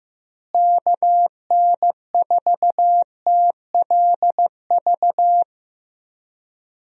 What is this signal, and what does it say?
Morse code, heard as a keyed tone, about 15 wpm, 700 Hz. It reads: KN4TLV
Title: USA Call Signs - 6 Characters - Repeated 2x - 15wpm